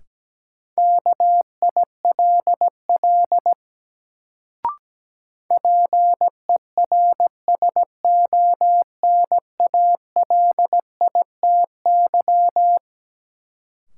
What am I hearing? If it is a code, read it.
Morse code, heard as a keyed tone, about 17 wpm, 700 Hz. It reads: KILL PERSONALITY
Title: Sentences from Top 1400 Words - New Word Drill - 17wpm